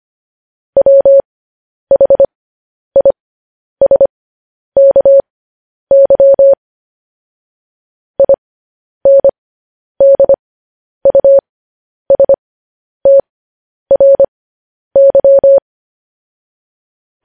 Morse code, heard as a keyed tone, about 25 wpm, spaced as for 10 wpm, 550 Hz. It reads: WHISKY INDUSTRY
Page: news headlines in morse code at 10 wpm